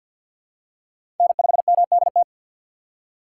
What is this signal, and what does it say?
Morse code, heard as a keyed tone, about 49 wpm, 700 Hz. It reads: N5MDT
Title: USA Call Signs - 5 Characters - 6x Character Spacing - 50wpm